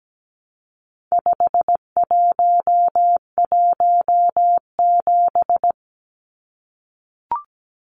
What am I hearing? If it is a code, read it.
Morse code, heard as a keyed tone, about 17 wpm, 700 Hz. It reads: 5117
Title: Numbers - 4-Digit 17wpm